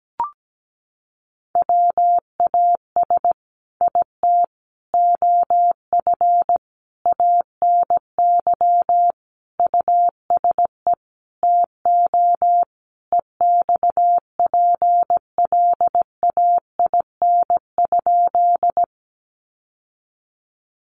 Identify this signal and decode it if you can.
Morse code, heard as a keyed tone, about 17 wpm, 700 Hz. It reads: WAS IT OF ANY USE TO EXPLAIN?
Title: Sentences from Top 1100 Words 17wpm